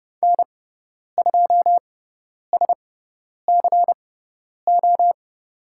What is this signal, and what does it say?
Morse code, heard as a keyed tone, about 30 wpm, spaced as for 10 wpm, 700 Hz. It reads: N2SCO
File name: Lesson 16g - Letter c - Call Signs 30wpm